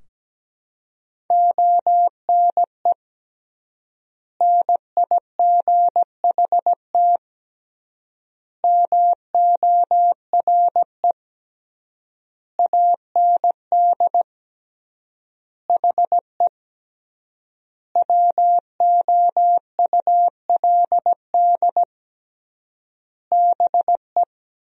Morse code, heard as a keyed tone, about 17 wpm, 700 Hz. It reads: ONE NIGHT MORE AND HE WOULD BE
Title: 3x Word-Spacing - Sentences from Top 300 Words - 17wpm